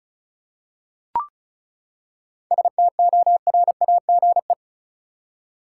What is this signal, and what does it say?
Morse code, heard as a keyed tone, about 34 wpm, 700 Hz. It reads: STORAGE
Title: Seven Letter Words - 35wpm